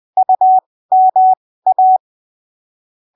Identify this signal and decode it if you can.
Morse code, heard as a keyed tone, about 20 wpm, spaced as for 15 wpm, 750 Hz. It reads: UMA
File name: arquivos / texto / texto 27 20ppm